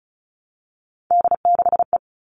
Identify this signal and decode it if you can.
Morse code, heard as a keyed tone, about 34 wpm, 700 Hz. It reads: D6E
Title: All Cut Numbers - 3-Digit 35wpm